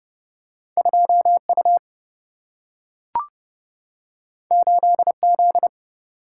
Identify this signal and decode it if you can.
Morse code, heard as a keyed tone, about 30 wpm, 700 Hz. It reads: 2 U 8 Z
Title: Single-Letter Single-Number 30wpm